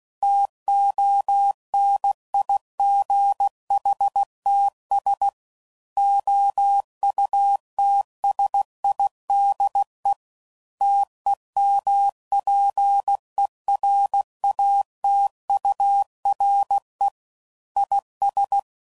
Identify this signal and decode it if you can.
Morse code, heard as a keyed tone, about 16 wpm, 800 Hz. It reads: TONIGHTS OUTSIDE TEMPERATURE IS
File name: CW Code Groups @15WPM / Weather Sets / Weather 5 12